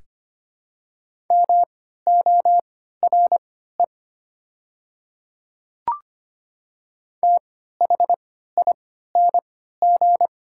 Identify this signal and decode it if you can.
Morse code, heard as a keyed tone, about 25 wpm, spaced as for 14 wpm, 700 Hz. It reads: MORE THING
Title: 3x Character Spacing - Top 500 Words - 25wpm